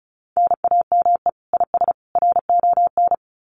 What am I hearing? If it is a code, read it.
Morse code, heard as a keyed tone, about 34 wpm, 700 Hz. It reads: NAME IS RON
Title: Common QSO Phrases 35wpm